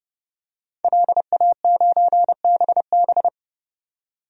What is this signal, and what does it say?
Morse code, heard as a keyed tone, about 32 wpm, 700 Hz. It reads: LA9BB
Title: DXCC-100 Call Signs - 5 Characters - 30wpm